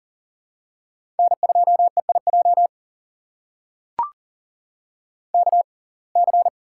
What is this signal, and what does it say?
Morse code, heard as a keyed tone, about 40 wpm, spaced as for 29 wpm, 700 Hz. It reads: N2EIJ KC
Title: USA Call Signs - 5 Characters - 6x Character Spacing - 40wpm